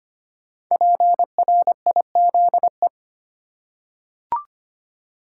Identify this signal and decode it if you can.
Morse code, heard as a keyed tone, about 25 wpm, 700 Hz. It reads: PRIZE
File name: 5 Letter Words - First 3 Letters Given - 25wpm